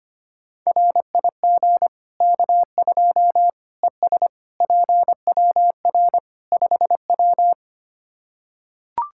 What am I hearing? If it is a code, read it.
Morse code, heard as a keyed tone, about 25 wpm, 700 Hz. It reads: RIG K2 ES PWR 5W
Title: Practice QSO - Exchange 2 - Info - 25wpm